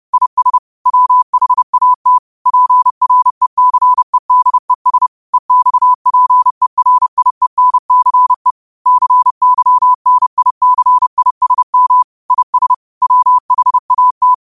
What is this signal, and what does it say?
Morse code, heard as a keyed tone, about 30 wpm, 1000 Hz. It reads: IS WHAT PRECEDES EXPERIENCE CYNICISM IS WHAT
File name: qotdfiles / 30wpm